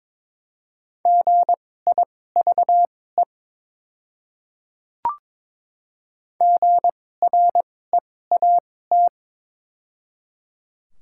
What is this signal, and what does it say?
Morse code, heard as a keyed tone, about 22 wpm, spaced as for 16 wpm, 690 Hz. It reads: GIVE GREAT
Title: 2x Character Spacing - Top 500 Words - 22wpm